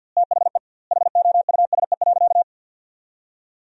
Morse code, heard as a keyed tone, about 52 wpm, 700 Hz. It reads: THE HOUSE.
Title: 2x Word-Spacing - Sentences from Top 400 Words - 50wpm